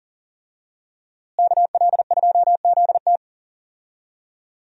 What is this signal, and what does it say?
Morse code, heard as a keyed tone, about 37 wpm, 700 Hz. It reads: KL2ZT
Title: USA Call Signs - 5 Characters - Repeated 2x - 40wpm